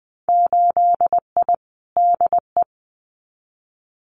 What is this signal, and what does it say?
Morse code, heard as a keyed tone, about 20 wpm, 700 Hz. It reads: 8I DE
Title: Winter Field Day Exchanges Encore 20wpm